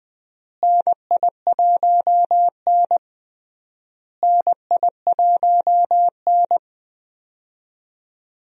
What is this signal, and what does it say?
Morse code, heard as a keyed tone, about 20 wpm, 700 Hz. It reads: NI1N NI1N
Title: USA Call Signs - 4 Characters - Repeated 3x - 20wpm